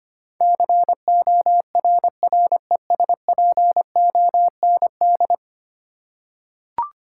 Morse code, heard as a keyed tone, about 25 wpm, 700 Hz. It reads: CORRESPOND
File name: 10 Letter Words - First 3 Letters Given - 25wpm